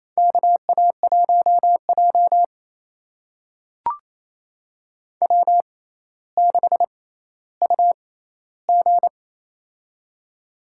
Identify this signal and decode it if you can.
Morse code, heard as a keyed tone, about 28 wpm, spaced as for 14 wpm, 700 Hz. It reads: KA1J W6UG
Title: USA Call Signs - 4 Characters - 6x Character Spacing - 28wpm